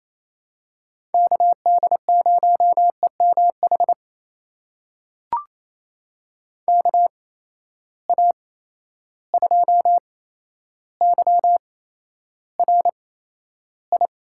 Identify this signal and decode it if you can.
Morse code, heard as a keyed tone, about 28 wpm, spaced as for 12 wpm, 700 Hz. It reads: KD0EMH KA2YRI
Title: USA Call Signs - 6 Characters - 8x Character Spacing - 28wpm